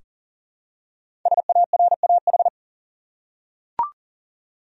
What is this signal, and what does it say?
Morse code, heard as a keyed tone, about 40 wpm, 700 Hz. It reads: SARAH
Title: USA Common Names - 40wpm